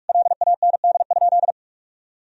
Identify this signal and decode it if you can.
Morse code, heard as a keyed tone, about 41 wpm, 700 Hz. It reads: LAND?
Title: Sentences from Top 1000 Words - Review Part 1 - 45wpm